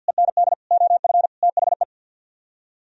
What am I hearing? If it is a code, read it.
Morse code, heard as a keyed tone, about 64 wpm, 700 Hz. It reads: END OF THE